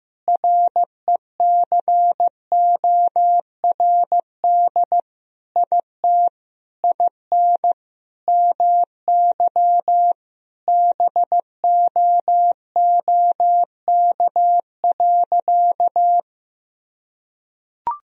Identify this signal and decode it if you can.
Morse code, heard as a keyed tone, about 15 wpm, 700 Hz. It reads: RECORD IT IN MY BOOK.
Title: Sentences from Top 1000 Words - Review Part 3 - 15wpm